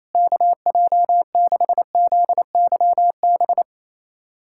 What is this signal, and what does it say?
Morse code, heard as a keyed tone, about 28 wpm, 700 Hz. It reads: KJ6ZYB
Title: USA Callsign DE Callsign - 6 Characters - 28wpm